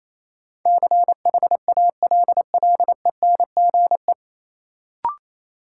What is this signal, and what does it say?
Morse code, heard as a keyed tone, about 28 wpm, 700 Hz. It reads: CHALLENGE
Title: Sentences from Top 1000 Words - Review Part 7 - New Word Drill - 28wpm